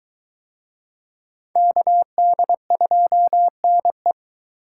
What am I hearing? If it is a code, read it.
Morse code, heard as a keyed tone, about 20 wpm, 700 Hz. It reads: KD2NE